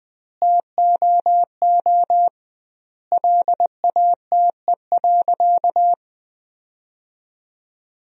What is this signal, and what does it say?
Morse code, heard as a keyed tone, about 20 wpm, 700 Hz. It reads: TOO LATE.
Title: 2x Word-Spacing - Sentences from Top 300 Words - 20wpm